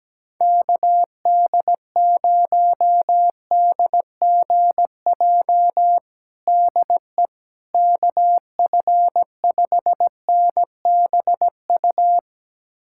Morse code, heard as a keyed tone, about 17 wpm, 700 Hz. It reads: KD0DGJ DE KF5NBU
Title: USA Callsign DE Callsign - 6 Characters - 17wpm